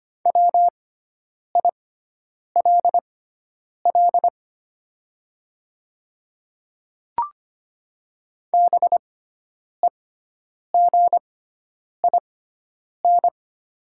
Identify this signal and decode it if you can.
Morse code, heard as a keyed tone, about 25 wpm, spaced as for 9 wpm, 700 Hz. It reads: WILL BEGIN
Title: 6x Character Spacing - Top 300 Words - 25wpm